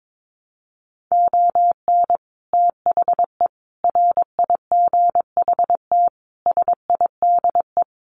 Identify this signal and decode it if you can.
Morse code, heard as a keyed tone, about 22 wpm, 700 Hz. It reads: ON THE RIGHT SIDE
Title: Sets of 5 Words Encore 22wpm